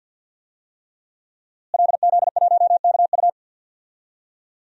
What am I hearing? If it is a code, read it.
Morse code, heard as a keyed tone, about 46 wpm, 700 Hz. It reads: LZ1KU
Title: DXCC-100 Call Signs - 5 Characters - Repeated 3x - 50wpm